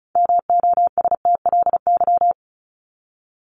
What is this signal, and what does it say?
Morse code, heard as a keyed tone, about 34 wpm, 700 Hz. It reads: MOSTLY
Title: Top 1001-1250 Words 35wpm